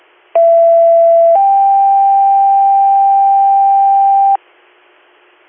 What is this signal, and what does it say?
Two-tone page, heard as a one-second tone, then a three-second tone, about 670 then 790 Hz. It is heard from 0.3 to 4.4 s.